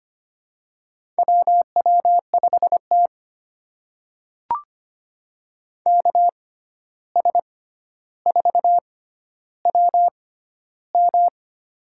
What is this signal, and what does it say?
Morse code, heard as a keyed tone, about 25 wpm, spaced as for 12 wpm, 700 Hz. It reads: WW5T KS4WM
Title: USA Call Signs - 6x Character Spacing - 25wpm